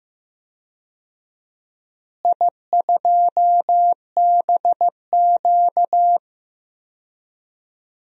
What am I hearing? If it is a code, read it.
Morse code, heard as a keyed tone, about 15 wpm, 700 Hz. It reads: I2BQ
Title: DXCC-100 Call Signs - 4 Characters - Repeated 3x - 15wpm